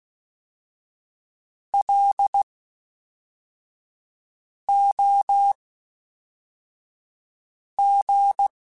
Morse code, heard as a keyed tone, about 16 wpm, spaced as for 4 wpm, 800 Hz. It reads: LOG
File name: (spaced) / FromSMs spaced mp3s / Classes@5